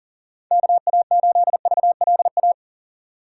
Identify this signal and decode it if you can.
Morse code, heard as a keyed tone, about 40 wpm, 700 Hz. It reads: KA8VLA
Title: USA Call Signs - 2x Character Spacing - 40wpm